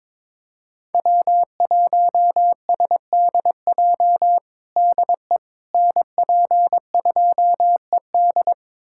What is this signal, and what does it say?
Morse code, heard as a keyed tone, about 22 wpm, 700 Hz. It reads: W1SDJ DE NP2ED
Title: USA Callsign DE Callsign - 5 Characters - 22wpm